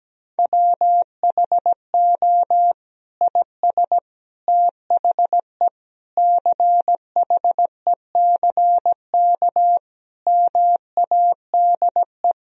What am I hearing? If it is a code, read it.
Morse code, heard as a keyed tone, about 17 wpm, 700 Hz. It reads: WHO IS THE CHECK MADE
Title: Sentences from Top 1000 Words - Review Part 3 - 17wpm